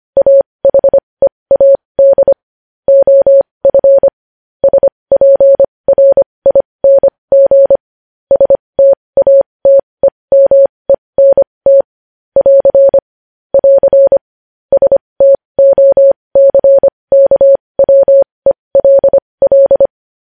Morse code, heard as a keyed tone, about 25 wpm, spaced as for 20 wpm, 550 Hz. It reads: AHEAD OF SPRING STATEMENT <AR> <AR> STOCKWELL
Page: news-headlines-in-morse-code-at-20-wpm